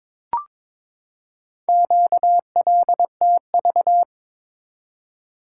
Morse code, heard as a keyed tone, about 22 wpm, 700 Hz. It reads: QLTV
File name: practice / 22wpm / Four Letter-Number Combinations - 22wpm